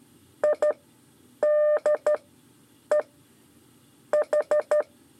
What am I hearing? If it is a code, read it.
Morse code, heard as a keyed tone, about 14 wpm, spaced as for 8 wpm, 600 Hz. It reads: ID E H
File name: 5 LEZIONE RIPASSO 2